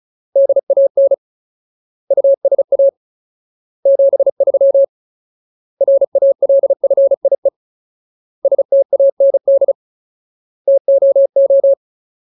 Morse code, heard as a keyed tone, about 34 wpm, 550 Hz. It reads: DAN USA 73 RALFIE STAND TOO
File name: Ultimate Word List / 35WPM Ultimate